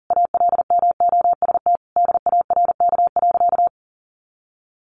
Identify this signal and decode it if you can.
Morse code, heard as a keyed tone, about 40 wpm, 700 Hz. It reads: ALMOST DARK.
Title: Sentences from Top 700 Words 40wpm